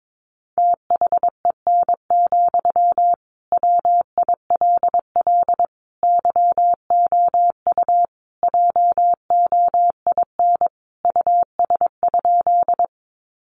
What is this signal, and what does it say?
Morse code, heard as a keyed tone, about 22 wpm, 700 Hz. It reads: THEN, WILL YOU JOIN US?
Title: Sentences from Top 1400 Words 22wpm